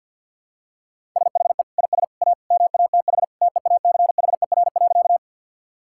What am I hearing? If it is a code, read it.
Morse code, heard as a keyed tone, about 50 wpm, 700 Hz. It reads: SHE IS A MATH TEACHER.
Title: Sentences from Top 1000 Words - Review Part 6 - 50wpm